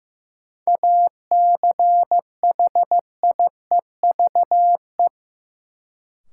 Morse code, heard as a keyed tone, about 15 wpm, 700 Hz. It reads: ACHIEVE
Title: Sentences from Top 1000 Words - Review Part 5 - New Word Drill - 15wpm